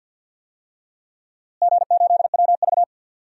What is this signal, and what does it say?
Morse code, heard as a keyed tone, about 52 wpm, 700 Hz. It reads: G8WV
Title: DXCC-100 Call Signs - 4 Characters - Repeated 3x - 50wpm